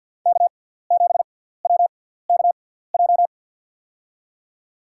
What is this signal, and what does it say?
Morse code, heard as a keyed tone, about 50 wpm, spaced as for 17 wpm, 700 Hz. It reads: K7WKJ
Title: USA Call Signs - 5 Characters - 6x Character Spacing - 50wpm